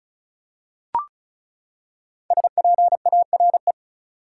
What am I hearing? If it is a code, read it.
Morse code, heard as a keyed tone, about 34 wpm, 700 Hz. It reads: SPARE
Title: Five Letter Words - 35wpm